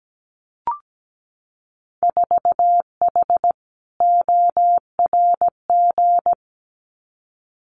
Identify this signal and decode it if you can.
Morse code, heard as a keyed tone, about 17 wpm, 700 Hz. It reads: 4H ORG
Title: Winter Field Day Exchanges Encore 17wpm